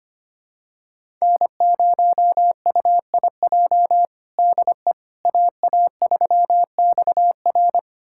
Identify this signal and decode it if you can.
Morse code, heard as a keyed tone, about 25 wpm, 700 Hz. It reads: N0UIJ DE AA3XR
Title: USA Callsign DE Callsign - 5 Characters - 25wpm